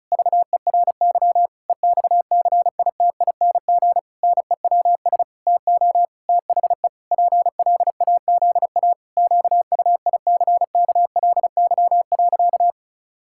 Morse code, heard as a keyed tone, about 36 wpm, 700 Hz. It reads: VERY EXCITING NEWS TO THE PLAZA QUICKLY.
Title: Pangram Challenge v2 - 35wpm